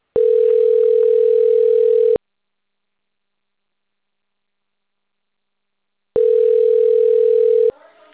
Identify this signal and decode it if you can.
Ringback tone (North America), 440 + 480 Hz.